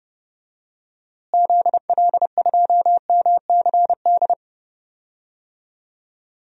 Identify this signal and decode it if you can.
Morse code, heard as a keyed tone, about 30 wpm, 700 Hz. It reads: ZL2MCD
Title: DXCC-100 Call Signs - 6 Characters - Repeated 2x - 30wpm